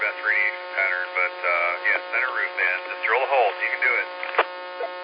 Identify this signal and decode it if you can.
Emergency Alert System (SAME) tones, heard as harsh buzzing bursts, with speech. It is heard from 0.7 to 4.1 s.